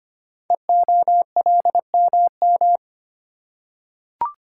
Morse code, heard as a keyed tone, about 25 wpm, 700 Hz. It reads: EOLMM
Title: Code Groups - 5 Letters - 25wpm